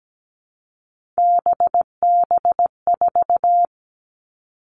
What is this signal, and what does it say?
Morse code, heard as a keyed tone, about 17 wpm, 700 Hz. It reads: BB4
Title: All Cut Numbers - 3-Digit 17wpm